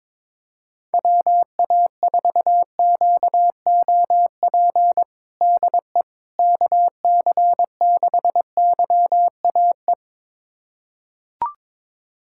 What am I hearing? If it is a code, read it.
Morse code, heard as a keyed tone, about 22 wpm, 700 Hz. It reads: WA4QOP DE KC6YAE
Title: USA Callsign DE Callsign 22wpm